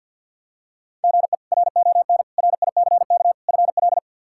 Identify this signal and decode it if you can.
Morse code, heard as a keyed tone, about 45 wpm, 700 Hz. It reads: GE RON RICK FL
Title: K1USN SST QSO Practice - 50wpm